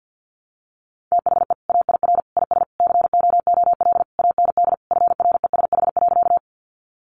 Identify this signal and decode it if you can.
Morse code, heard as a keyed tone, about 50 wpm, 700 Hz. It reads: THE AIR IS COOL AND FRESH.
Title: Sentences from Top 1200 Words 50wpm